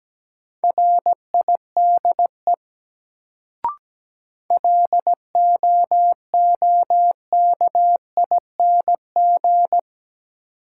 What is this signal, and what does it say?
Morse code, heard as a keyed tone, about 17 wpm, 700 Hz. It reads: RIDE LOOKING
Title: Sentences from Top 1000 Words - Review Part 6 - New Word Drill - 17wpm